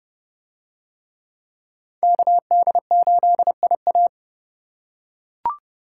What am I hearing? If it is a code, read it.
Morse code, heard as a keyed tone, about 29 wpm, 700 Hz. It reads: KD8IA